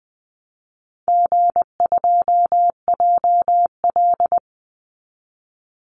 Morse code, heard as a keyed tone, about 20 wpm, 700 Hz. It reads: G2JL
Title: Call Signs - 20wpm